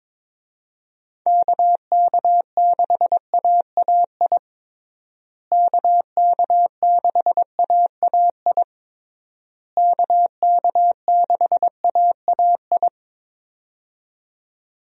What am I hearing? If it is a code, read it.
Morse code, heard as a keyed tone, about 22 wpm, 700 Hz. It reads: KK6AAI KK6AAI KK6AAI
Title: USA Call Signs - 6 Characters - Repeated 3x - 22wpm